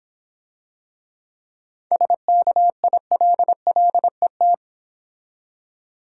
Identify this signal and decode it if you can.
Morse code, heard as a keyed tone, about 26 wpm, 690 Hz. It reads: SKILLET